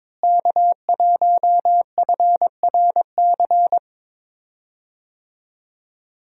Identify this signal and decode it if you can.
Morse code, heard as a keyed tone, about 22 wpm, 700 Hz. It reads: K1FRC
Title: USA Call Signs - 5 Characters - Repeated 2x - 22wpm